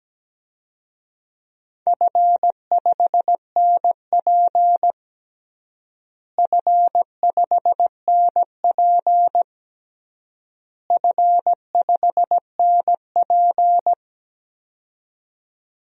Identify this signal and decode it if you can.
Morse code, heard as a keyed tone, about 17 wpm, 700 Hz. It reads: F5NP F5NP F5NP
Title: DXCC-100 Call Signs - 4 Characters - Repeated 3x - 17wpm